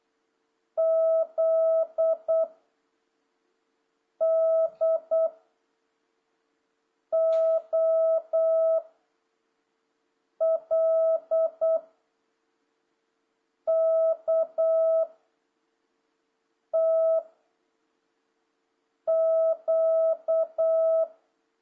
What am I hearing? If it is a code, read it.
Morse code, heard as a keyed tone, about 8 wpm, 660 Hz. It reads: Z D O L K T Q
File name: jh8cbh / 2026cw